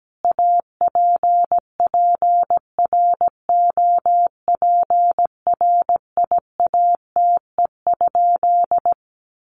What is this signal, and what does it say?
Morse code, heard as a keyed tone, about 17 wpm, 700 Hz. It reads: APPROPRIATE?
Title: Sentences from Top 900 Words 17wpm